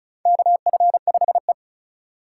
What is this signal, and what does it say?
Morse code, heard as a keyed tone, about 36 wpm, 700 Hz. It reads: KF5E
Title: USA Call Signs - 4 Characters - 3x Character Spacing - 35wpm